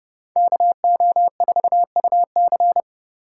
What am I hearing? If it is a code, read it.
Morse code, heard as a keyed tone, about 39 wpm, 700 Hz. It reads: KO4UC